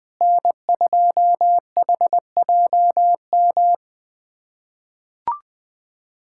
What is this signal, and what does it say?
Morse code, heard as a keyed tone, about 20 wpm, 700 Hz. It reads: N2HJM
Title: USA Callsign DE Callsign - 5 Characters - 20wpm